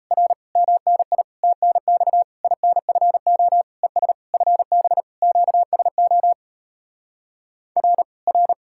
Morse code, heard as a keyed tone, about 37 wpm, 700 Hz. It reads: R MNI TNX INFO ES FB QSO R R